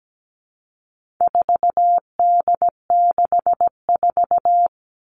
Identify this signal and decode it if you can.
Morse code, heard as a keyed tone, about 17 wpm, 700 Hz. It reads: 4D64
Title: All Cut Numbers - 4-Digit 17wpm